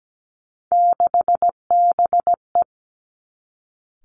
Morse code, heard as a keyed tone, about 17 wpm, 700 Hz. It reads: 6BE
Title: Three Challenging Letter-Number Combinations 17wpm